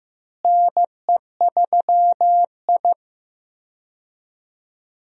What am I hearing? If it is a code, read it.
Morse code, heard as a keyed tone, about 15 wpm, 700 Hz. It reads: NE3I
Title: USA Call Signs - 4 Characters - Repeated 3x - 15wpm